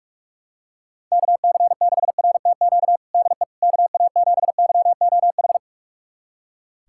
Morse code, heard as a keyed tone, about 46 wpm, 700 Hz. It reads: KC6RTQ DE KA7YOH
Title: USA Callsign DE Callsign - 6 Characters - 45wpm